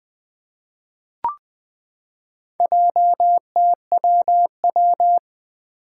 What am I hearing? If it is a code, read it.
Morse code, heard as a keyed tone, about 20 wpm, 700 Hz. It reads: JTWW
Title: Four Challenging Letter-Number Combinations 20wpm